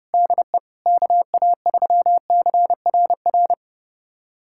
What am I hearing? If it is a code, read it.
Morse code, heard as a keyed tone, about 30 wpm, 700 Hz. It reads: DE KA3CRR
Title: USA Callsign DE Callsign - 6 Characters - 30wpm